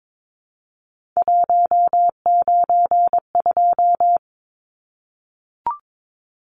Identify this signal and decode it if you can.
Morse code, heard as a keyed tone, about 22 wpm, 700 Hz. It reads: 192
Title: Numbers - 3-Digit 22wpm